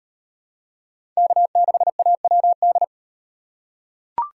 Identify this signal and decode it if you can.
Morse code, heard as a keyed tone, about 38 wpm, 700 Hz. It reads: K6AWD